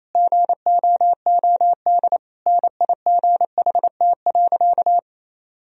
Morse code, heard as a keyed tone, about 28 wpm, 700 Hz. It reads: GOOD NIGHT.
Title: Sentences from Top 1000 Words - Review Part 4 - 28wpm